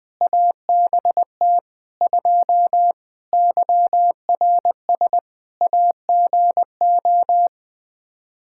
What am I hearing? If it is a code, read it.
Morse code, heard as a keyed tone, about 20 wpm, 700 Hz. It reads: ABT 2 YRS AGO
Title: K1USN SST QSO Practice 20wpm